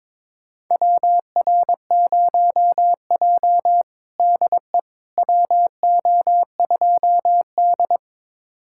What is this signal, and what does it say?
Morse code, heard as a keyed tone, about 22 wpm, 700 Hz. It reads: WR0J DE WO2D
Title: USA Callsign DE Callsign - 4 Characters - 22wpm